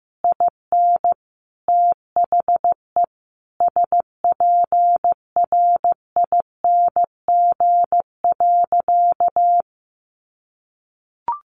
Morse code, heard as a keyed tone, about 15 wpm, 700 Hz. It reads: IN THE SPRING.
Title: Sentences from Top 1200 Words 15wpm